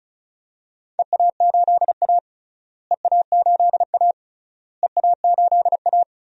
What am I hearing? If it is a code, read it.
Morse code, heard as a keyed tone, about 36 wpm, 700 Hz. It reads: EA8A EA8A EA8A
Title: DXCC-100 Call Signs - 4 Characters - Repeated 3x - 35wpm